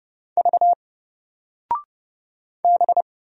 Morse code, heard as a keyed tone, about 30 wpm, 700 Hz. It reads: V B
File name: Single-Letter-Number-30wpm